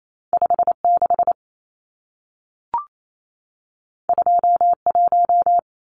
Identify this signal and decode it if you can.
Morse code, heard as a keyed tone, about 28 wpm, 700 Hz. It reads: 56 21
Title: Numbers - 2-Digit 28wpm